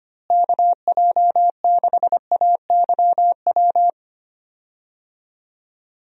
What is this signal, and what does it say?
Morse code, heard as a keyed tone, about 25 wpm, 700 Hz. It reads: KJ6AYW
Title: USA Call Signs - 6 Characters - Repeated 3x - 25wpm